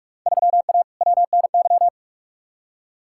Morse code, heard as a keyed tone, about 46 wpm, 700 Hz. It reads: 3A WNY
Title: Field Day Exchanges - 45wpm